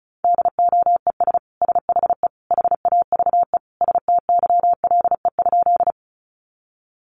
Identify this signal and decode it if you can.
Morse code, heard as a keyed tone, about 34 wpm, 700 Hz. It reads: DOES SHE HAVE STYLE?
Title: Sentences from Top 800 Words 35wpm